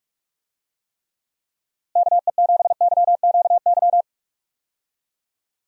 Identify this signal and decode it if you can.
Morse code, heard as a keyed tone, about 46 wpm, 700 Hz. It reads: KE7YQY
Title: USA Call Signs - 6 Characters - Repeated 3x - 45wpm